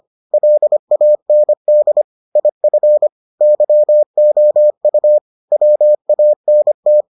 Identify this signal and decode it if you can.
Morse code, heard as a keyed tone, about 25 wpm, 600 Hz. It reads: LAND IF YOU WANT